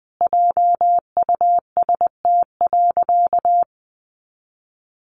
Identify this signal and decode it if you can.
Morse code, heard as a keyed tone, about 20 wpm, 700 Hz. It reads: JUST.
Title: Sentences from Top 800 Words 20wpm